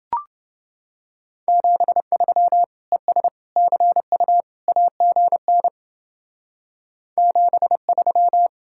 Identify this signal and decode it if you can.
Morse code, heard as a keyed tone, about 28 wpm, 700 Hz. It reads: 73 ES CU AGN 73